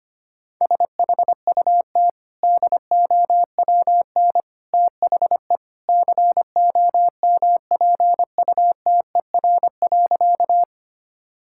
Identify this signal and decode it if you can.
Morse code, heard as a keyed tone, about 25 wpm, 700 Hz. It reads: SHUT DOWN THE COMPUTER.
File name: Easy Sentences with Hints - 25wpm